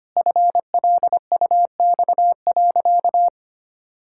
Morse code, heard as a keyed tone, about 25 wpm, 700 Hz. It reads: FLUX.
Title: Pangram Challenge v2 - 25wpm